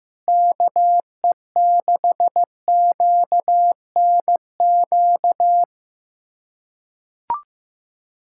Morse code, heard as a keyed tone, about 15 wpm, 700 Hz. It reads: KE6QNQ
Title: USA Callsign DE Callsign - 6 Characters - 15wpm